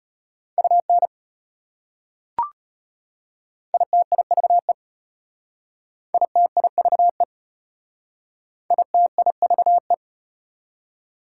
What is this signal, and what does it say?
Morse code, heard as a keyed tone, about 33 wpm, 700 Hz. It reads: UN ITIVE ITIVE ITIVE